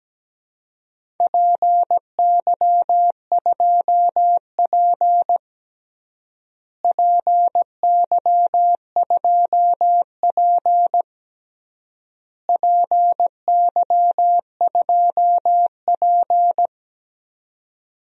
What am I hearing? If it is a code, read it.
Morse code, heard as a keyed tone, about 17 wpm, 700 Hz. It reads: PY2P PY2P PY2P
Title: DXCC-100 Call Signs - 4 Characters - Repeated 3x - 17wpm